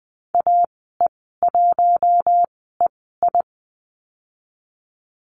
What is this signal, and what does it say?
Morse code, heard as a keyed tone, about 20 wpm, spaced as for 15 wpm, 700 Hz. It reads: AE1EI
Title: 076 Lesson 13g - Letter L - Call Signs 20wpm